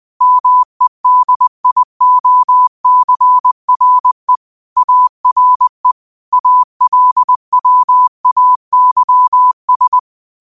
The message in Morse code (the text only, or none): MEDIOCRE ARE ALWAYS